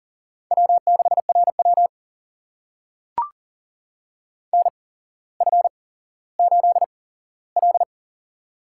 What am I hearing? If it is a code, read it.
Morse code, heard as a keyed tone, about 40 wpm, spaced as for 17 wpm, 700 Hz. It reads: W6RW NF8L
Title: USA Call Signs - 8x Character Spacing - 40wpm